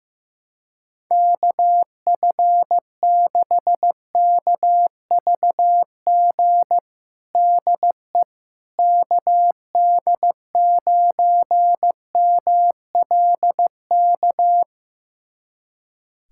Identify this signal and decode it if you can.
Morse code, heard as a keyed tone, about 15 wpm, 700 Hz. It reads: KF6KVG DE KD9MLK
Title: USA Callsign DE Callsign 15wpm